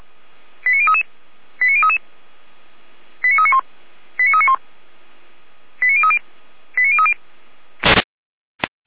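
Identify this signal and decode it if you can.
Five-tone selective call (ZVEI), heard as quick runs of five tones, tones of about 70 ms. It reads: 89033 89033 89391 89391 89030 89030
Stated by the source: ZVEI five-tone selective call received on an SDR.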